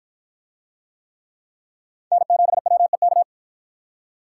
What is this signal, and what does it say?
Morse code, heard as a keyed tone, about 53 wpm, 700 Hz. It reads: N7WEK